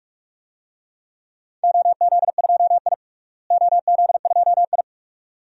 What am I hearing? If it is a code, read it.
Morse code, heard as a keyed tone, about 42 wpm, 700 Hz. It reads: OZ2I OZ2I
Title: DXCC-100 Call Signs - 4 Characters - Repeated 2x - 45wpm